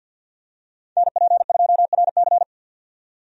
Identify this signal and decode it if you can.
Morse code, heard as a keyed tone, about 48 wpm, 700 Hz. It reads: NP2RC
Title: USA Call Signs - 5 Characters - 50wpm